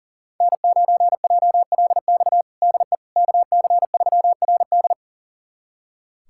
Morse code, heard as a keyed tone, about 40 wpm, 700 Hz. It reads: N9JLX DE KC3RD
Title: USA Callsign DE Callsign 40wpm